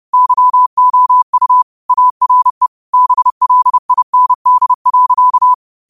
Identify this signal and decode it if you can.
Morse code, heard as a keyed tone, about 30 wpm, 1000 Hz. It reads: YOU ARE BLIND.